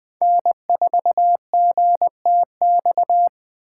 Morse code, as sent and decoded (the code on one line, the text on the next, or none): -. ....- --. - -..-
N4GTX